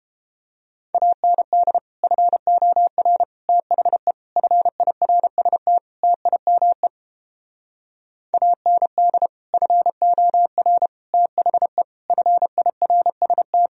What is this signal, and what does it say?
Morse code, heard as a keyed tone, about 31 wpm, 700 Hz. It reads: AND FOR THE FIRST TIME AND FOR THE FIRST